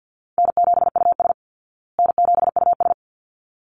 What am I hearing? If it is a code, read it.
Morse code, heard as a keyed tone, about 50 wpm, 700 Hz. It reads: N7US N7US
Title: Call Signs Repeated 3x - 50wpm